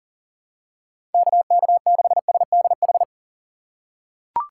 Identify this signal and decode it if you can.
Morse code, heard as a keyed tone, about 40 wpm, 700 Hz. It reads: KK6SDH
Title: USA Call Signs - 6 Characters - 2x Character Spacing - 40wpm